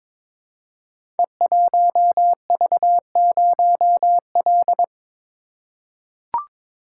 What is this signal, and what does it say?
Morse code, heard as a keyed tone, about 22 wpm, 700 Hz. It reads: E1V0L
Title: Code Groups - 5 Letter:Numbers - 22wpm